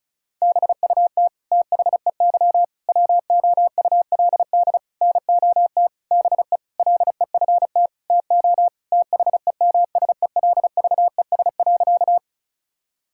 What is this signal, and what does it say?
Morse code, heard as a keyed tone, about 36 wpm, 700 Hz. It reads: BUT THEY WOULD NOT BE LEFT TO THEMSELVES.
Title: Sentences from Top 1000 Words - Review Part 5 - 35wpm